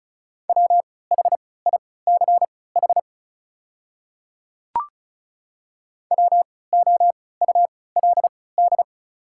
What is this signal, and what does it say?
Morse code, heard as a keyed tone, about 36 wpm, spaced as for 20 wpm, 690 Hz. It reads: WHICH WOULD
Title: 3x Character Spacing - Top 100 Words - 35wpm